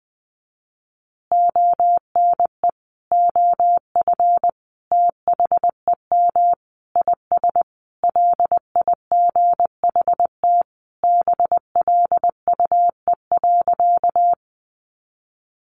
Morse code, heard as a keyed tone, about 19 wpm, 700 Hz. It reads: ONE OF THEM IS LIGHT BLUE.
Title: Additional Easy Phrases 20wpm